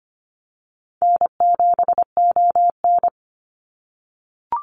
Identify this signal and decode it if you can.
Morse code, heard as a keyed tone, about 25 wpm, 700 Hz. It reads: N7ON
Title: Easy Call Signs - 25wpm